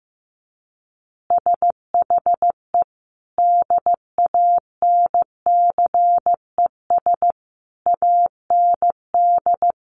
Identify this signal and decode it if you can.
Morse code, heard as a keyed tone, about 14 wpm, 700 Hz. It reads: SHE DANCES AND
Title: Easy Phrases 15wpm